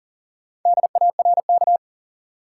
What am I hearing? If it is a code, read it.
Morse code, heard as a keyed tone, about 40 wpm, 700 Hz. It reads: DARK
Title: Four Letter Words - 40wpm